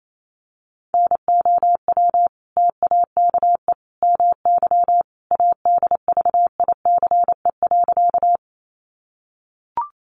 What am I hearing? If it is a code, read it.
Morse code, heard as a keyed tone, about 28 wpm, 700 Hz. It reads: NOW TAKE MY ADVICE.
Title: Sentences from Top 1100 Words 28wpm